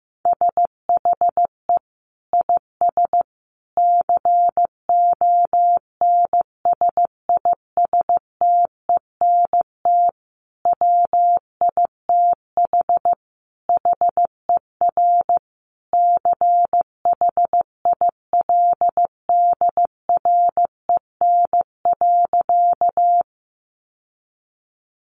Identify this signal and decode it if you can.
Morse code, heard as a keyed tone, about 15 wpm, 700 Hz. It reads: SHE IS CONSISTENT WITH HER CHILDREN.
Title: Sentences from Top 1500 Words 15wpm